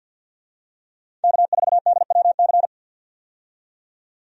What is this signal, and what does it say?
Morse code, heard as a keyed tone, about 52 wpm, 700 Hz. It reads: K4DWC
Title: USA Call Signs - 5 Characters - 50wpm